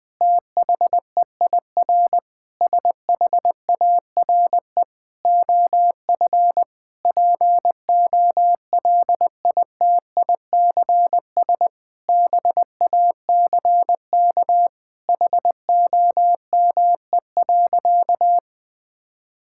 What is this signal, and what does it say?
Morse code, heard as a keyed tone, about 20 wpm, 700 Hz. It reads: THEIR SHARE OF POLITICS BACK HOME.
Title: Sentences from Top 1000 Words - Review Part 6 - 20wpm